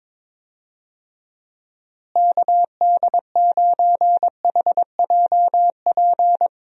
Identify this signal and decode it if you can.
Morse code, heard as a keyed tone, about 22 wpm, 700 Hz. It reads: KD9HJP
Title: USA Call Signs - 6 Characters - Repeated 2x - 22wpm